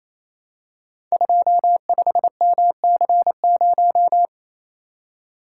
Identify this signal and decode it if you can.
Morse code, heard as a keyed tone, about 29 wpm, 700 Hz. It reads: 25MC0
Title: Code Groups - 5 Letter:Numbers - 28wpm